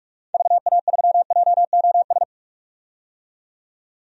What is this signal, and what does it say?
Morse code, heard as a keyed tone, about 44 wpm, 700 Hz. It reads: VA3JOS
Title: DXCC-100 Call Signs - 6 Characters - 45wpm